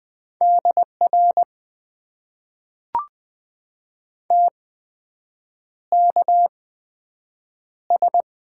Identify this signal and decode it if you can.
Morse code, heard as a keyed tone, about 20 wpm, spaced as for 7 wpm, 700 Hz. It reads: DR TKS
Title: Top 90 QSO Elements - 8x Character Spacing - 20wpm